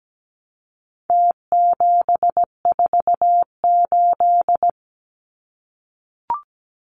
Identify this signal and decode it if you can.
Morse code, heard as a keyed tone, about 17 wpm, 700 Hz. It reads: T748
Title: Numbers - Contest Style - 4-Digit 17wpm